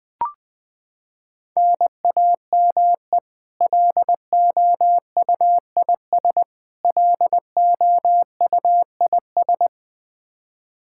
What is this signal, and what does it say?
Morse code, heard as a keyed tone, about 21 wpm, 700 Hz. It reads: NAME LOUIS LOUIS
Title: K1USN SST QSO Practice 20wpm